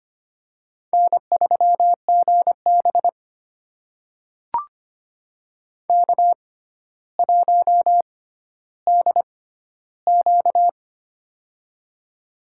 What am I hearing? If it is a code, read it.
Morse code, heard as a keyed tone, about 25 wpm, spaced as for 13 wpm, 700 Hz. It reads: N3GB K1DQ
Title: USA Call Signs - 4 Characters - 6x Character Spacing - 25wpm